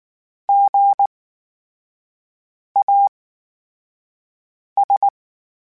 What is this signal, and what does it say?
Morse code, heard as a keyed tone, about 19 wpm, spaced as for 5 wpm, 800 Hz. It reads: GAS
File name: Special INGs / Special INGs-mp3s / VerySlowToFaster Ing @18